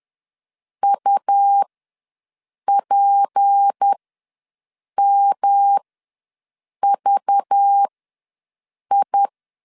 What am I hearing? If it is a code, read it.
Morse code, heard as a keyed tone, about 10 wpm, spaced as for 6 wpm, 800 Hz. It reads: UPMVI
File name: LEZIONI / 6 PKX0 / 6 P ws 10wpm 3751